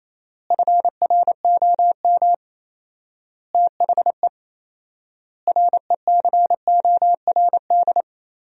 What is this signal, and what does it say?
Morse code, heard as a keyed tone, about 28 wpm, 700 Hz. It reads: FROM THE RECORD
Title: 4x Word-Spacing - Sentences from Top 500 Words - 28wpm